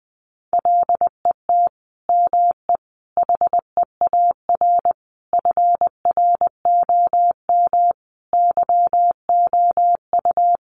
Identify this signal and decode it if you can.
Morse code, heard as a keyed tone, about 20 wpm, 700 Hz. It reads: LET ME HEAR FROM YOU
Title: Sentences from Top 400 Words 20wpm